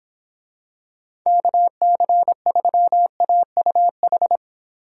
Morse code, heard as a keyed tone, about 25 wpm, 700 Hz. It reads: KC3AUH